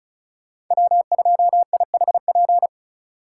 Morse code, heard as a keyed tone, about 34 wpm, 700 Hz. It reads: W2IHP
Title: USA Call Signs - 5 Characters - 3x Character Spacing - 35wpm